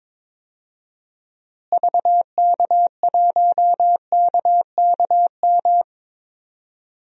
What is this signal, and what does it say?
Morse code, heard as a keyed tone, about 22 wpm, 700 Hz. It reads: VK1KKM
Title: DXCC-100 Call Signs - 6 Characters - Repeated 3x - 22wpm